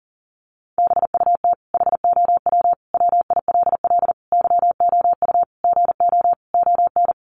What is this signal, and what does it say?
Morse code, heard as a keyed tone, about 40 wpm, 700 Hz. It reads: BUT HOW WILL YOU GO ON
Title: Sentences from Top 1100 Words 40wpm